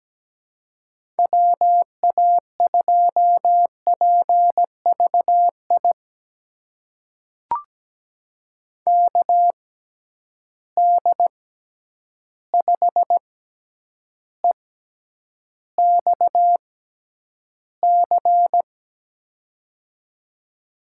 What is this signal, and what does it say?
Morse code, heard as a keyed tone, about 17 wpm, spaced as for 9 wpm, 700 Hz. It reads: WA2PVI KD5EXC
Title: USA Call Signs - 6 Characters - 6x Character Spacing - 17wpm